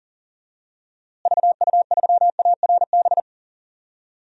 Morse code, heard as a keyed tone, about 40 wpm, 700 Hz. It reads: VU3ARB
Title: DXCC-100 Call Signs - 6 Characters - Repeated 2x - 40wpm